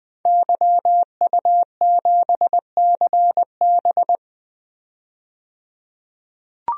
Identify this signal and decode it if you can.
Morse code, heard as a keyed tone, about 21 wpm, 700 Hz. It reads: YU7CB
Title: DXCC-100 Call Signs - 5 Characters - 20wpm